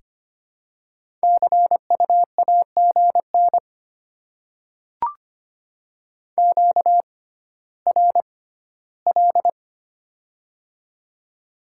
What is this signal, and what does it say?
Morse code, heard as a keyed tone, about 25 wpm, spaced as for 14 wpm, 700 Hz. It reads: CU AGN QRL
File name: Top 20 QSO Elements - 6x Character Spacing - 25wpm